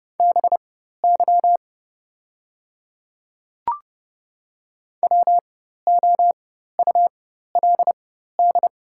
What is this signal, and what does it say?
Morse code, heard as a keyed tone, about 31 wpm, spaced as for 14 wpm, 690 Hz. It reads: BY WOULD
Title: 4x Character Spacing - Top 100 Words - 30wpm